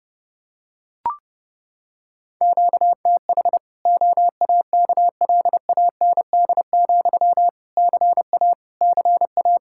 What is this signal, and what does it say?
Morse code, heard as a keyed tone, about 30 wpm, 700 Hz. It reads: QTH OAKLAND, CA CA
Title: Practice QSO - Exchange 1 - Beginning - 30wpm